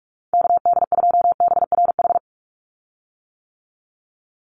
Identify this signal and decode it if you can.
Morse code, heard as a keyed tone, about 44 wpm, 700 Hz. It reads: KD2BRH
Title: USA Call Signs 45wpm